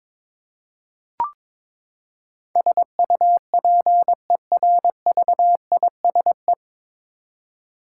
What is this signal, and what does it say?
Morse code, heard as a keyed tone, about 22 wpm, 700 Hz. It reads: SUPERVISE
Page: Morse Code Ninja practice 22wpm